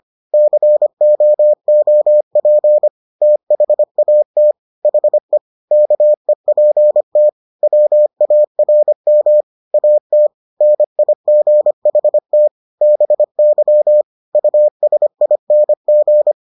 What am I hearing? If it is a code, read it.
Morse code, heard as a keyed tone, about 25 wpm, 600 Hz. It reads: COOP THAT HE KEPT WARM AT NIGHT BY USING